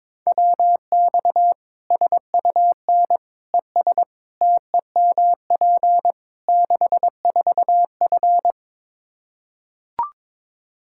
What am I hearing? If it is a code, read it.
Morse code, heard as a keyed tone, about 22 wpm, 700 Hz. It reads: WX SUN ES TEMP 64F
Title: Practice QSO - Exchange 2 - Info - 22wpm